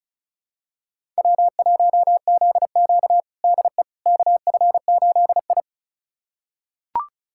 Morse code, heard as a keyed tone, about 34 wpm, 700 Hz. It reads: W1ZQ DE KF8I
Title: USA Callsign DE Callsign - 4 Characters - 35wpm